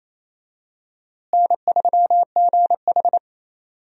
Morse code, heard as a keyed tone, about 28 wpm, 700 Hz. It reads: N3GH
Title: USA Call Signs - 4 Characters - Repeated 3x - 28wpm